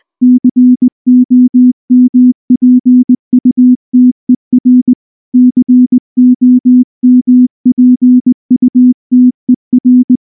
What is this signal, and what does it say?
Morse code, heard as a keyed tone, about 20 wpm, 250 Hz. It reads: COMPUTER COMPUTER
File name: morse-training-20WPM